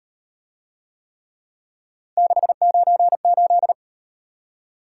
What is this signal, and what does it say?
Morse code, heard as a keyed tone, about 35 wpm, 700 Hz. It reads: 698